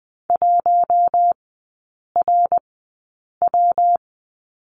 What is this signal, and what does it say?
Morse code, heard as a keyed tone, about 20 wpm, spaced as for 8 wpm, 700 Hz. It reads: 1RW
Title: 121 Lesson 20e - Number 3 - Three Characters 20wpm